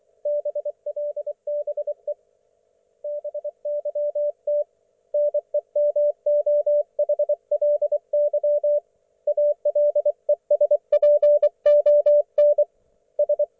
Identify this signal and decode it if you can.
Morse code, heard as a keyed tone, about 24 wpm, 590 Hz. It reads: BLBE BYT NEMOHLY ALESPON S
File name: Archiv / CUC212 / ok1fhi cul212